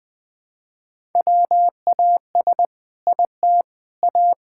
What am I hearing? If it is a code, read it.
Morse code, heard as a keyed tone, about 21 wpm, 700 Hz. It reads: WAS IT A
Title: Sentences from Top 1000 Words - Review Part 6 - 20wpm